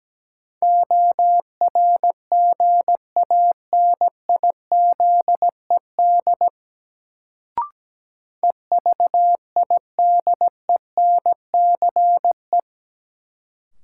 Morse code, heard as a keyed tone, about 17 wpm, 700 Hz. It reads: ORGANIZED EVIDENCE
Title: Sentences from Top 1300 Words - New Word Drill - 17wpm